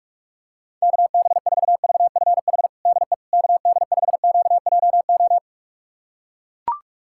Morse code, heard as a keyed tone, about 42 wpm, 700 Hz. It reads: KB4VFH DE KD5QJO
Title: USA Callsign DE Callsign 45wpm